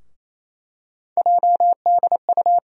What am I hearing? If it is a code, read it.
Morse code, heard as a keyed tone, about 28 wpm, 700 Hz. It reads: JDU
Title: USA Call Sign Postfixes - 3 Letters 28wpm